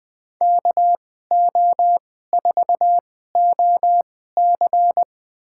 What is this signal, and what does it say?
Morse code, heard as a keyed tone, about 21 wpm, spaced as for 15 wpm, 700 Hz. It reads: KO4OC
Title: USA Call Signs - 5 Characters - 2x Character Spacing - 20wpm